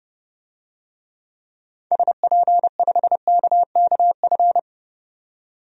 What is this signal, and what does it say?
Morse code, heard as a keyed tone, about 32 wpm, 700 Hz. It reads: SP5KKF
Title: DXCC-100 Call Signs - 6 Characters - 30wpm